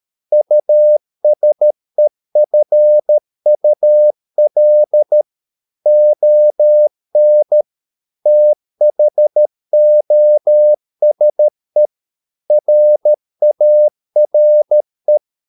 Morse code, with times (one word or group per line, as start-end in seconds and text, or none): 0.3-5.2 USEFUL
5.9-7.6 ON
8.3-11.9 THOSE
12.5-15.2 RARE